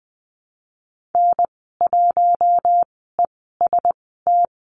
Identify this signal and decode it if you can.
Morse code, heard as a keyed tone, about 20 wpm, spaced as for 14 wpm, 700 Hz. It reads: N1EST
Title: 042 Lesson 8g - Number 1 - Call Signs 20wpm